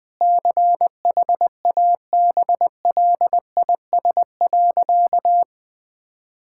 Morse code, heard as a keyed tone, about 20 wpm, 700 Hz. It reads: CHABLIS.
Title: Pangram Challenge v2 - 20wpm